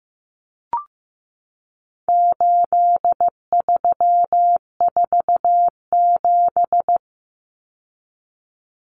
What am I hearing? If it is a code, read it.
Morse code, heard as a keyed tone, about 15 wpm, 700 Hz. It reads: 8347
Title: Numbers - 4-Digit 15wpm